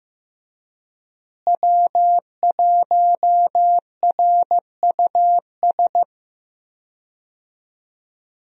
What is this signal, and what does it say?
Morse code, heard as a keyed tone, about 15 wpm, 700 Hz. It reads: W1RUS
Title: USA Call Signs - 5 Characters - Repeated 2x - 15wpm